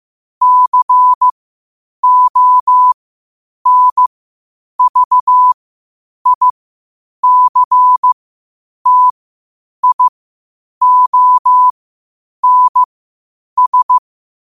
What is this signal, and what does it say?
Morse code, heard as a keyed tone, about 15 wpm, spaced as for 8 wpm, 1000 Hz. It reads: CONVICTIONS
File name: qotdfiles / 8wpm